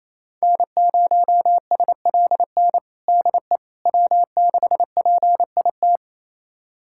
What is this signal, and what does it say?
Morse code, heard as a keyed tone, about 28 wpm, 700 Hz. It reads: N0SLN DE W6PIT
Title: USA Callsign DE Callsign - 5 Characters - 28wpm